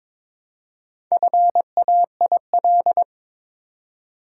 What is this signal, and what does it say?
Morse code, heard as a keyed tone, about 23 wpm, 700 Hz. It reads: FAIL
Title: Four Letter Words - 22wpm